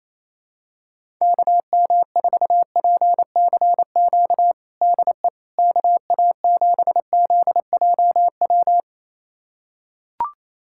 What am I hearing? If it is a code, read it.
Morse code, heard as a keyed tone, about 28 wpm, 700 Hz. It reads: KM4PCQ DE KA7ZJW
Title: USA Callsign DE Callsign - 6 Characters - 28wpm